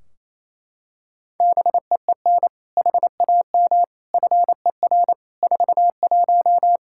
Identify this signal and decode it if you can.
Morse code, heard as a keyed tone, about 27 wpm, 700 Hz. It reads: BEEN HAM FER 41
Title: K1USN SST QSO Practice 28wpm